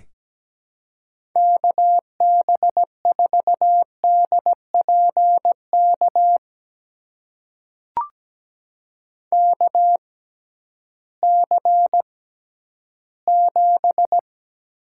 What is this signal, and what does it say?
Morse code, heard as a keyed tone, about 17 wpm, spaced as for 11 wpm, 700 Hz. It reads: KB4DPK KC7
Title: USA Call Signs - 6 Characters - 6x Character Spacing - 17wpm